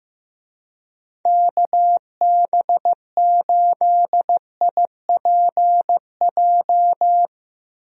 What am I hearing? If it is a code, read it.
Morse code, heard as a keyed tone, about 15 wpm, 700 Hz. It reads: KB8IPJ
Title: USA Call Signs - 6 Characters - Repeated 3x - 15wpm